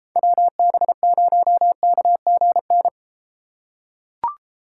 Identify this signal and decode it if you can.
Morse code, heard as a keyed tone, about 33 wpm, 700 Hz. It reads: WB0KGN